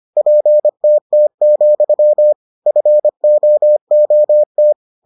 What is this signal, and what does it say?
Morse code, heard as a keyed tone, about 25 wpm, 600 Hz. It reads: PTT, FOOT